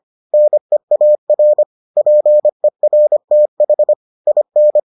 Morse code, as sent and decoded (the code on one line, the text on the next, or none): -. .- .-. / .--. .-. - .... / .. -.
NEAR PERTH IN